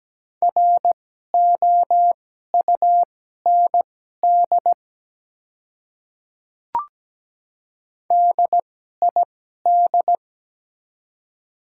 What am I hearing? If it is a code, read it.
Morse code, heard as a keyed tone, about 17 wpm, spaced as for 12 wpm, 700 Hz. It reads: ROUND DID
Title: 2x Character Spacing - Top 300 Words - 17wpm